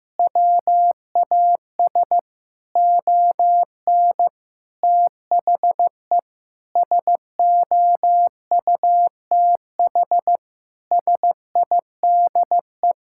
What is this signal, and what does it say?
Morse code, heard as a keyed tone, about 15 wpm, 700 Hz. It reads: WAS ON THE SOUTH SIDE
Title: Sentences from Top 1000 Words - Review Part 5 - 15wpm